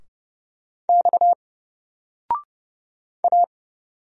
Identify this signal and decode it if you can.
Morse code, heard as a keyed tone, about 30 wpm, 700 Hz. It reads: X A
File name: Single-Letter-Number-30wpm